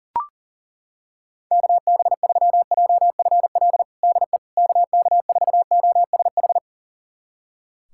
Morse code, heard as a keyed tone, about 39 wpm, 700 Hz. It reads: KB3JFL DE KK4OSH
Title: USA Callsign DE Callsign 40wpm